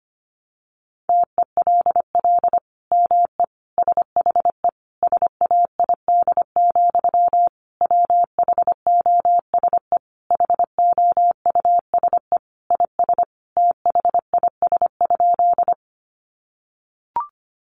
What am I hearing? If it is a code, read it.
Morse code, heard as a keyed tone, about 25 wpm, 700 Hz. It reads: TELL ME SHE SAID, WHOSE HOUSE IS THIS?
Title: Sentences from Top 1200 Words 25wpm